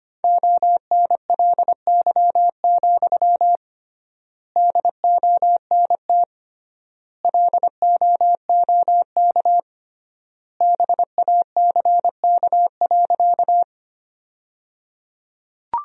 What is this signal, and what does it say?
Morse code, heard as a keyed tone, about 25 wpm, 700 Hz. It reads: ONLY, DONT LOOK BACK.
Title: 3x Word-Spacing - Sentences from Top 300 Words - 25wpm